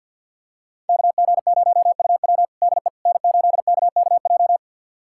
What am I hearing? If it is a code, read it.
Morse code, heard as a keyed tone, about 47 wpm, 700 Hz. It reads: KG0UW DE N8KKJ
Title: USA Callsign DE Callsign - 5 Characters - 50wpm